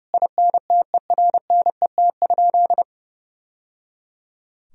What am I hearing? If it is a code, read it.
Morse code, heard as a keyed tone, about 30 wpm, 700 Hz. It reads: INTERNET?
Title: Sentences from Top 1000 Words - Review Part 4 - 30wpm